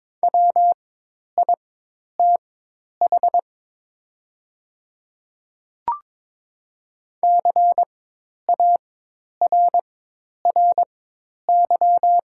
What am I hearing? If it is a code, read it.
Morse code, heard as a keyed tone, about 22 wpm, spaced as for 10 wpm, 700 Hz. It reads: WITH CARRY
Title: 4x Character Spacing - Top 300 Words - 22wpm